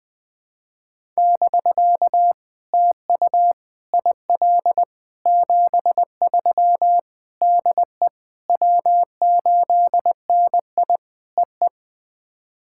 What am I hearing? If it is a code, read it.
Morse code, heard as a keyed tone, about 20 wpm, 700 Hz. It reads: <BK> TU IL 73 DE W8NI EE
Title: POTA QSO Practice - 20wpm